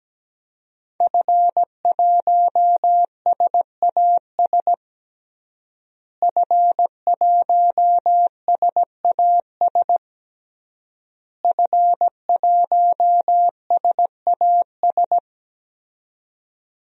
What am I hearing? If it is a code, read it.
Morse code, heard as a keyed tone, about 17 wpm, 700 Hz. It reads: F1SAS F1SAS F1SAS
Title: DXCC-100 Call Signs - 5 Characters - Repeated 3x - 17wpm